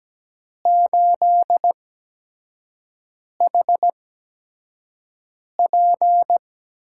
Morse code, heard as a keyed tone, about 17 wpm, spaced as for 5 wpm, 700 Hz. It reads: 8HP
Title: USA Call Signs - 5 Characters - 8x Character Spacing - 17wpm